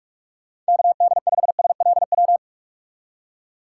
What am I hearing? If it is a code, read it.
Morse code, heard as a keyed tone, about 46 wpm, 700 Hz. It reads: KD5SLW
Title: USA Call Signs - 2x Character Spacing - 45wpm